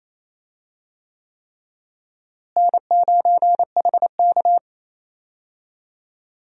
Morse code, heard as a keyed tone, about 28 wpm, 700 Hz. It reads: N9HK
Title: USA Call Signs - 4 Characters - Repeated 3x - 28wpm